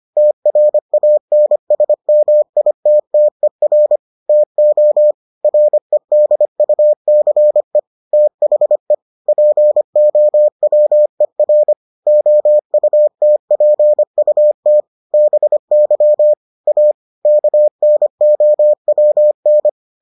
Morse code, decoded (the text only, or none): TRANSMITTER TO REDUCE THE POWER OUTPUT BY A KNOWN